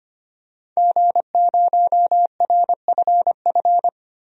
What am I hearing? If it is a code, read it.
Morse code, heard as a keyed tone, about 25 wpm, 700 Hz. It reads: G0RFF
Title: DXCC-100 Call Signs - 5 Characters - 25wpm